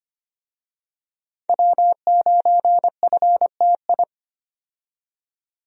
Morse code, heard as a keyed tone, about 25 wpm, 700 Hz. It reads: W9FTI
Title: USA Call Signs - 5 Characters - 25wpm